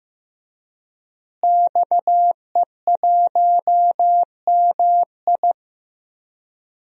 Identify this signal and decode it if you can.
Morse code, heard as a keyed tone, about 15 wpm, 700 Hz. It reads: XE1MI
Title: DXCC-100 Call Signs - 5 Characters - Repeated 2x - 15wpm